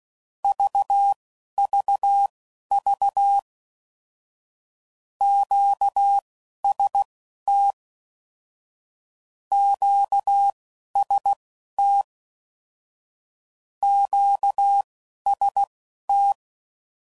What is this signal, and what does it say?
Morse code, heard as a keyed tone, about 16 wpm, spaced as for 11 wpm, 800 Hz. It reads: VVV QST QST QST
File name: Basic Sets / QST de WA7RSO / Combnations